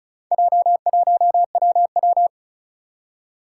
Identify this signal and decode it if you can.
Morse code, heard as a keyed tone, about 36 wpm, 700 Hz. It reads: J1WW
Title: Four Challenging Letter-Number Combinations 35wpm